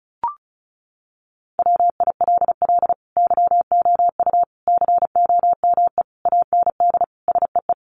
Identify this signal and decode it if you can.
Morse code, heard as a keyed tone, about 34 wpm, 700 Hz. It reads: WILL YOU COME AND SEE
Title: Sentences from Top 100 Words 35wpm